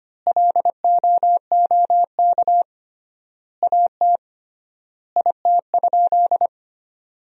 Morse code, heard as a keyed tone, about 25 wpm, 700 Hz. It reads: LOOK AT IT?
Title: 3x Word-Spacing - Sentences from Top 100 Words - 25wpm